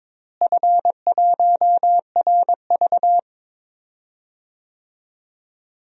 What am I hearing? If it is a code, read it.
Morse code, heard as a keyed tone, about 22 wpm, 700 Hz. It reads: F1RV
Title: DXCC-100 Call Signs - 4 Characters - Repeated 3x - 22wpm